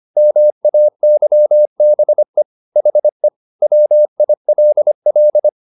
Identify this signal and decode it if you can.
Morse code, heard as a keyed tone, about 25 wpm, 600 Hz. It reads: MAYBE HE WILL
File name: the old man and the sea part 1 25wpm